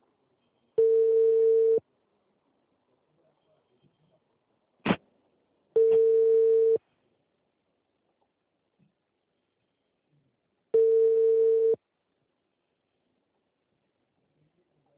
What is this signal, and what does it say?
Ringback tone (China, 450 Hz).